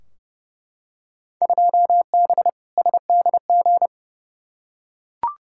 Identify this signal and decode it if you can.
Morse code, heard as a keyed tone, about 30 wpm, 700 Hz. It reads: 2B SDG
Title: Field Day Exchanges Encore - 30wpm